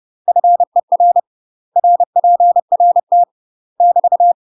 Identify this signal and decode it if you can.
Morse code, heard as a keyed tone, about 30 wpm, 700 Hz. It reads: FER RPRT <BT>
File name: UK MASTER QSOs TRAINING / 30WPM / UK MASTER QSOs 0045